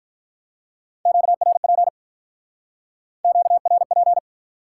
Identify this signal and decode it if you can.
Morse code, heard as a keyed tone, about 47 wpm, 700 Hz. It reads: QRP QRP